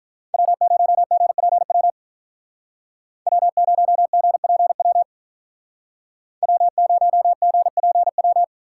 Morse code, heard as a keyed tone, about 44 wpm, 700 Hz. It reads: W0GPW W0GPW W0GPW